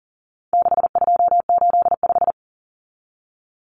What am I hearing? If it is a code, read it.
Morse code, heard as a keyed tone, about 40 wpm, 700 Hz. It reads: 6285
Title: Common Cut Numbers - 4-Digit 40wpm